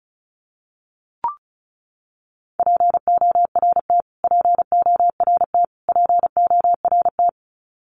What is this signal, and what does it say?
Morse code, heard as a keyed tone, about 34 wpm, 700 Hz. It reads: PORT PORT PORT
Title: Top 500 Words Repeated 35wpm